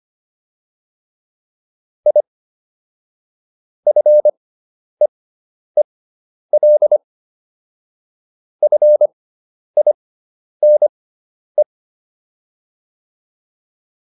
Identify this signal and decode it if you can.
Morse code, heard as a keyed tone, about 25 wpm, spaced as for 10 wpm, 600 Hz. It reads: I FEEL FINE